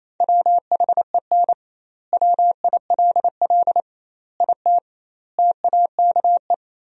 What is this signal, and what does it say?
Morse code, heard as a keyed tone, about 27 wpm, 700 Hz. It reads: WHEN WILL IT TAKE
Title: 2x Word-Spacing - Sentences from Top 200 Words - 28wpm